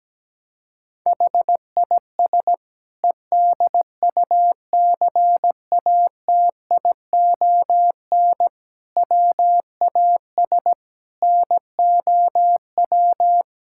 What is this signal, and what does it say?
Morse code, heard as a keyed tone, about 17 wpm, 700 Hz. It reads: HIS EDUCATION WAS NOW
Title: Sentences from Top 1000 Words - Review Part 6 - 17wpm